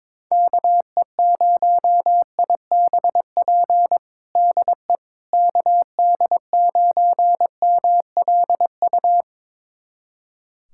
Morse code, heard as a keyed tone, about 22 wpm, 700 Hz. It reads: KE0IBP DE KD9MLU
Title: USA Callsign DE Callsign - 6 Characters - 22wpm